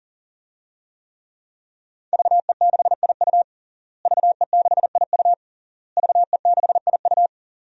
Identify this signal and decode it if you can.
Morse code, heard as a keyed tone, about 40 wpm, 700 Hz. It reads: VE6IU VE6IU VE6IU
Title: DXCC-100 Call Signs - 5 Characters - Repeated 3x - 40wpm